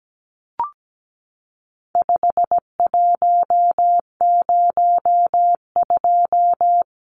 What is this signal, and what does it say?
Morse code, heard as a keyed tone, about 17 wpm, 700 Hz. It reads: 5102
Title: Numbers - 4-Digit 17wpm